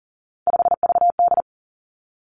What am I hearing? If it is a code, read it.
Morse code, heard as a keyed tone, about 39 wpm, 700 Hz. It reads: 5VD
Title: Three Challenging Letter-Number Combinations 40wpm